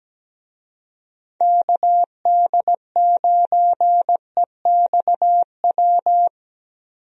Morse code, heard as a keyed tone, about 17 wpm, 700 Hz. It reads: KD9EXW
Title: USA Call Signs - 6 Characters - 17wpm